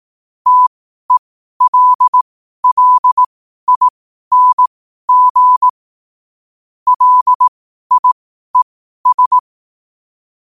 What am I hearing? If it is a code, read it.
Morse code, heard as a keyed tone, about 18 wpm, spaced as for 12 wpm, 1000 Hz. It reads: TELLING LIES
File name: qotdfiles / 12wpm